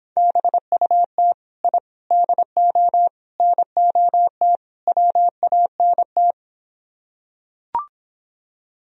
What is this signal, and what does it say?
Morse code, heard as a keyed tone, about 26 wpm, 700 Hz. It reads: BUT I DO NOT WANT